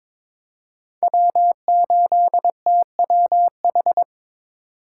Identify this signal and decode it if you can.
Morse code, heard as a keyed tone, about 22 wpm, 700 Hz. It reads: W8TWH
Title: USA Call Signs - 5 Characters - 3x Character Spacing - 22wpm